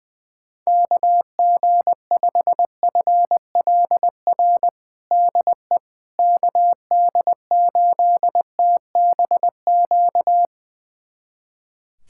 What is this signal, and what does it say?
Morse code, heard as a keyed tone, about 21 wpm, 700 Hz. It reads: KG5FLR DE KD8TBQ
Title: USA Callsign DE Callsign 20wpm